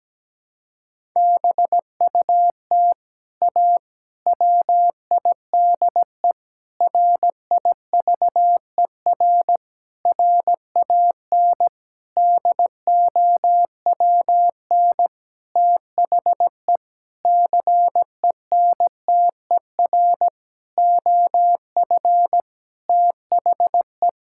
Morse code, heard as a keyed tone, about 17 wpm, 700 Hz. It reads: BUT A WIDE RIVER RAN DOWN THE CENTER OF THE
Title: Sentences from Top 1000 Words - Review Part 6 - 17wpm